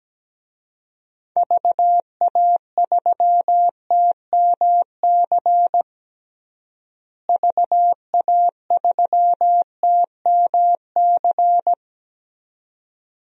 Morse code, heard as a keyed tone, about 17 wpm, 700 Hz. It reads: VA3TMC VA3TMC
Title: DXCC-100 Call Signs - 6 Characters - Repeated 2x - 17wpm